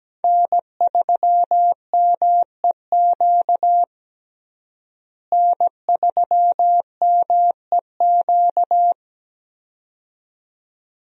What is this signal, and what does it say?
Morse code, heard as a keyed tone, about 17 wpm, 700 Hz. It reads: N3MEQ N3MEQ
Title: USA Call Signs - 5 Characters - Repeated 2x - 17wpm